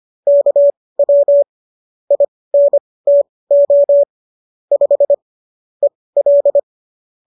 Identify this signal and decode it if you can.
Morse code, heard as a keyed tone, about 25 wpm, spaced as for 18 wpm, 570 Hz. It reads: KW INTO 5 EL
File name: qso218 18